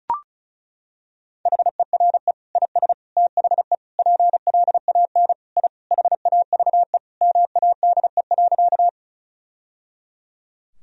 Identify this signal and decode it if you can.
Morse code, heard as a keyed tone, about 34 wpm, 700 Hz. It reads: HERE IS THE PLAN I HAVE MADE.
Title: Sentences from Top 1000 Words - Review Part 2 - 35wpm